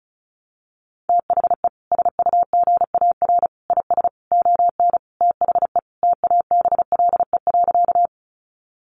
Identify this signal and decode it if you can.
Morse code, heard as a keyed tone, about 34 wpm, 700 Hz. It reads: THE SUGAR IS ON THE TABLE.
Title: Sentences from Top 1200 Words 35wpm